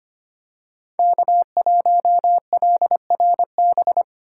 Morse code, heard as a keyed tone, about 25 wpm, 700 Hz. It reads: K1LRB
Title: USA Call Signs - 5 Characters - 3x Character Spacing - 25wpm